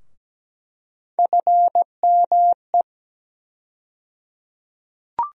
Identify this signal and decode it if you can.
Morse code, heard as a keyed tone, about 17 wpm, 700 Hz. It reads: FME
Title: USA Call Sign Postfixes - 3 Letters 17wpm